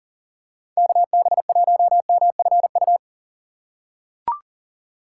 Morse code, heard as a keyed tone, about 38 wpm, 700 Hz. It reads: KB1MFU
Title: USA Call Signs - 6 Characters - 6x Character Spacing - 40wpm